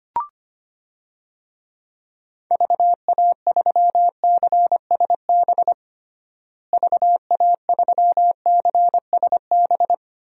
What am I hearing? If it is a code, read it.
Morse code, heard as a keyed tone, about 25 wpm, 700 Hz. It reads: VA3CSB VA3CSB
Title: DXCC-100 Call Signs - 6 Characters - Repeated 3x - 25wpm